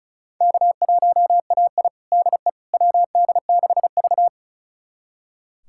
Morse code, heard as a keyed tone, about 34 wpm, 700 Hz. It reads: K1AI DE WD6V
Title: USA Callsign DE Callsign - 4 Characters - 35wpm